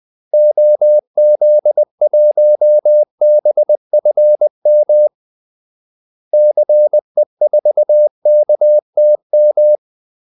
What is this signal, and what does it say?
Morse code, heard as a keyed tone, about 20 wpm, 600 Hz. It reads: OZ1BFM CE4KTM
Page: Kaldesignaler 20 wpm